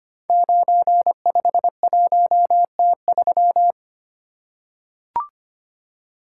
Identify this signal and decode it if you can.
Morse code, heard as a keyed tone, about 25 wpm, 690 Hz. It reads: 951T3
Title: Code Groups - 5 Letter:Numbers - 25wpm